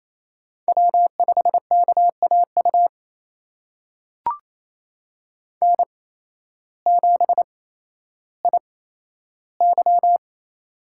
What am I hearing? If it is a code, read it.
Morse code, heard as a keyed tone, about 28 wpm, spaced as for 13 wpm, 700 Hz. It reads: W5KAU N7IY
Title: USA Call Signs - 8x Character Spacing - 28wpm